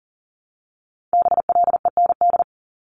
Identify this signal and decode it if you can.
Morse code, heard as a keyed tone, about 37 wpm, 700 Hz. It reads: BLEND
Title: Multi-letter Phonemes 40wpm